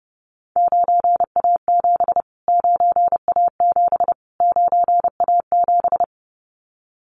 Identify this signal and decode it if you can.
Morse code, heard as a keyed tone, about 30 wpm, 700 Hz. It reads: 9A7 9A7 9A7
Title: Callsign Prefixes for Top 25 DXCC - Repeated 3x - 30wpm